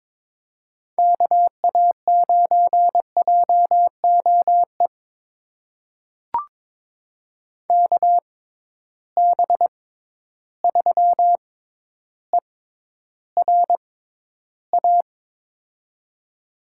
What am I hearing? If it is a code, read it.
Morse code, heard as a keyed tone, about 22 wpm, spaced as for 11 wpm, 700 Hz. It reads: KA9JOE KB3ERA
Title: USA Call Signs - 6x Character Spacing - 22wpm